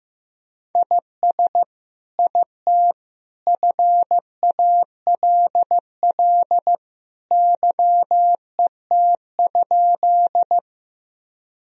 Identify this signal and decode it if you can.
Morse code, heard as a keyed tone, about 16 wpm, 700 Hz. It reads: IS IT FALL YET?
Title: Sentences from Top 1000 Words - Review Part 2 - 15wpm